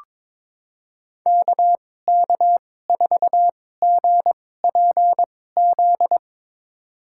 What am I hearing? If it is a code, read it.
Morse code, heard as a keyed tone, about 22 wpm, spaced as for 16 wpm, 700 Hz. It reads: KK4GPZ
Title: USA Call Signs - 6 Characters - 2x Character Spacing - 22wpm